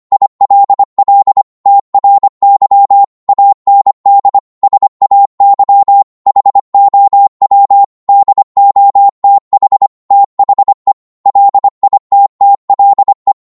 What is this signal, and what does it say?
Morse code, heard as a keyed tone, about 25 wpm, 800 Hz. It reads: ILL TRY AND SAY HOW DOTH THE LITTLE